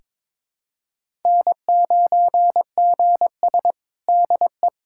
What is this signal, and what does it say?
Morse code, heard as a keyed tone, about 22 wpm, 700 Hz. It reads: N9GS DE
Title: USA Callsign DE Callsign - 4 Characters - 22wpm